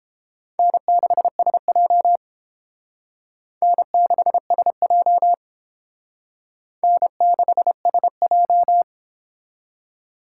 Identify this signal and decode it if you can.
Morse code, heard as a keyed tone, about 29 wpm, 700 Hz. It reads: N6SJ N6SJ N6SJ